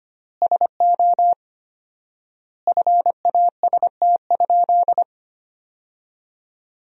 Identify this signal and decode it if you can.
Morse code, heard as a keyed tone, about 25 wpm, 700 Hz. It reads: SO FAST?
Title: 4x Word-Spacing - Sentences from Top 500 Words - 25wpm